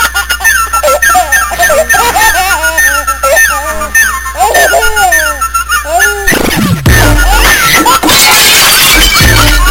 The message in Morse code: none